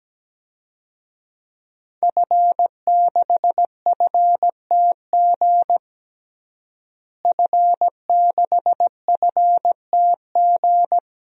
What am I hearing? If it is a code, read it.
Morse code, heard as a keyed tone, about 17 wpm, 700 Hz. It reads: F6FTG F6FTG
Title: DXCC-100 Call Signs - 5 Characters - Repeated 2x - 17wpm